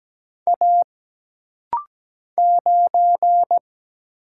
Morse code, heard as a keyed tone, about 17 wpm, 700 Hz. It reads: A 9
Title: Single Letter-Number Alt - Rapid-Fire 17wpm